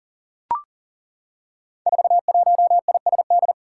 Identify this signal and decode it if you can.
Morse code, heard as a keyed tone, about 40 wpm, 700 Hz. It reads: 41ISD
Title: Five Challenging Letter-Number Combinations 40wpm